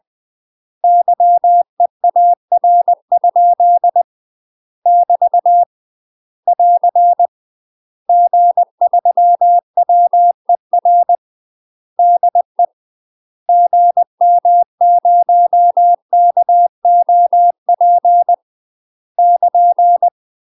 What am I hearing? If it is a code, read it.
Morse code, heard as a keyed tone, about 20 wpm, 700 Hz. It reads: YEAR? <BT> <AR> G3WER DE GM0KOP <KN>
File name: UK MASTER QSOs TRAINING / 20WPM / UK MASTER QSOs 0062